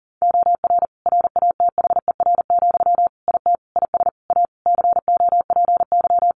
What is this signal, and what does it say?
Morse code, heard as a keyed tone, about 37 wpm, 700 Hz. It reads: OR RATHER, IT IS A COPY
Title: Sentences from Top 1400 Words 40wpm